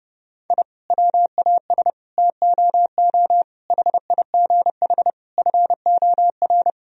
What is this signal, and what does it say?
Morse code, heard as a keyed tone, about 30 wpm, 700 Hz. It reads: I WAS TOO HIGH FOR
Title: Sentences from Top 1000 Words - Review Part 1 - 30wpm